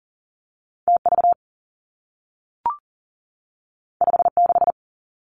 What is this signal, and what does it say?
Morse code, heard as a keyed tone, about 42 wpm, 700 Hz. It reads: T V 5 6
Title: Two Challenging Letter-Number Combinations 40wpm